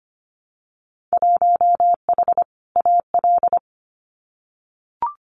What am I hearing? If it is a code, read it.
Morse code, heard as a keyed tone, about 25 wpm, 700 Hz. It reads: 1H AL
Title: Winter Field Day Exchanges 25wpm